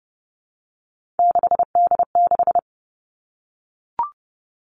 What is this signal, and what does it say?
Morse code, heard as a keyed tone, about 30 wpm, 700 Hz. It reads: 6D6
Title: Three Challenging Letter-Number Combinations 30wpm